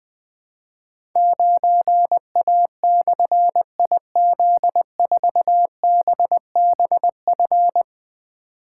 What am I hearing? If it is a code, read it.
Morse code, heard as a keyed tone, about 20 wpm, 700 Hz. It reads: 9A/IZ4BBF
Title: Difficult Call Signs 20wpm